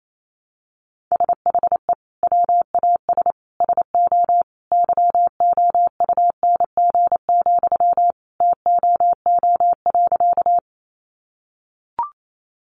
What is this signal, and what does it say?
Morse code, heard as a keyed tone, about 28 wpm, 700 Hz. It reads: SHE WAS SO YOUNG, TOO.
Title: Sentences from Top 400 Words 28wpm